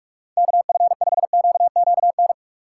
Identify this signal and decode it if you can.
Morse code, heard as a keyed tone, about 42 wpm, 700 Hz. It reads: KF5QQN
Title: USA Call Signs - 8x Character Spacing - 45wpm